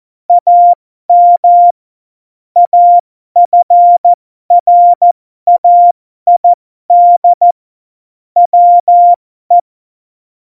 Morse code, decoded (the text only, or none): AM AFRAID WE